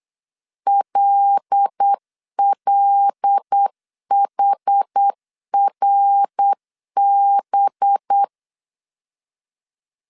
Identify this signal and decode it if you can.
Morse code, heard as a keyed tone, about 8 wpm, 800 Hz. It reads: LLHRB
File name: CW / DET / 1 6 1500C S B 8wpm 5142